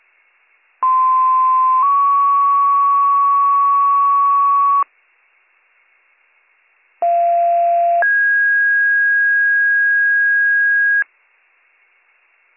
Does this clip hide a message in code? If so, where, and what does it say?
Yes: two-tone page from 0.8 to 11.0 s, heard as a one-second tone, then a three-second tone, about 1040 then 1120 Hz, about 690 then 1650 Hz.